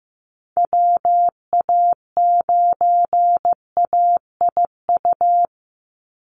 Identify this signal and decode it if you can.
Morse code, heard as a keyed tone, about 15 wpm, 700 Hz. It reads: WA9AIU
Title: USA Call Signs 15wpm